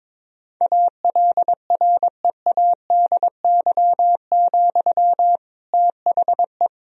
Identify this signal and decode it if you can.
Morse code, heard as a keyed tone, about 21 wpm, 700 Hz. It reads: ALREADY, THE
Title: Sentences from Top 1000 Words - Review Part 4 - 22wpm